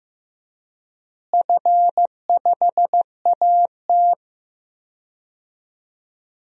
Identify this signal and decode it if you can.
Morse code, heard as a keyed tone, about 15 wpm, 700 Hz. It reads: F5AT
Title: DXCC-100 Call Signs - 4 Characters - Repeated 2x - 15wpm